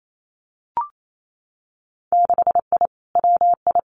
Morse code, heard as a keyed tone, about 28 wpm, 700 Hz. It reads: 6I WI
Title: Winter Field Day Exchanges Encore 28wpm